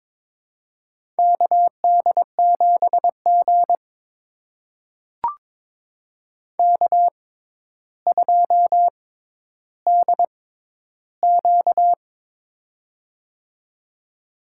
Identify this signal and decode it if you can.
Morse code, heard as a keyed tone, about 22 wpm, spaced as for 11 wpm, 700 Hz. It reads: KD7G K2DQ
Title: USA Call Signs - 6x Character Spacing - 22wpm